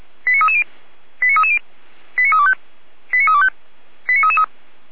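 Five-tone selective call (ZVEI), heard as quick runs of five tones, tones of about 70 ms. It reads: 89330 89330 89325 89325 89333
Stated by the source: ZVEI five-tone selective call, code 89319, received by SDR and served by feuerwehr-orlamuende (fire brigade Orlamünde).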